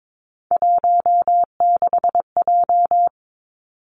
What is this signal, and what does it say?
Morse code, heard as a keyed tone, about 22 wpm, 700 Hz. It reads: 16J
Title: Three Challenging Letter-Number Combinations 22wpm